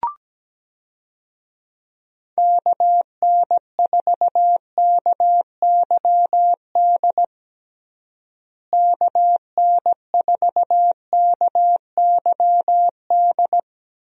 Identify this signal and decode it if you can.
Morse code, heard as a keyed tone, about 17 wpm, 700 Hz. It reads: KN4KYD KN4KYD
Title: USA Call Signs - 6 Characters - Repeated 2x - 17wpm